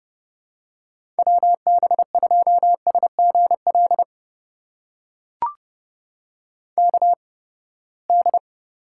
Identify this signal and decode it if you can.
Morse code, heard as a keyed tone, about 30 wpm, spaced as for 21 wpm, 700 Hz. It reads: WB2SGL KD
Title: USA Call Signs - 6 Characters - 8x Character Spacing - 30wpm